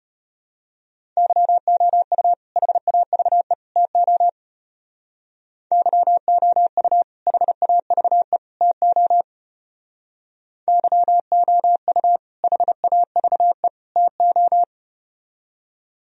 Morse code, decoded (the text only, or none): YOU HAVE TO YOU HAVE TO YOU HAVE TO